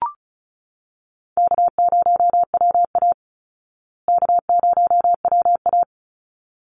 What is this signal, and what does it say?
Morse code, heard as a keyed tone, about 34 wpm, 700 Hz. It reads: K0WA K0WA
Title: Call Signs Repeated 3x - 35wpm